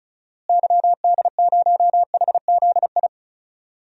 Morse code, heard as a keyed tone, about 36 wpm, 700 Hz. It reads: YD0HZI
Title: DXCC-100 Call Signs - 6 Characters - Repeated 3x - 35wpm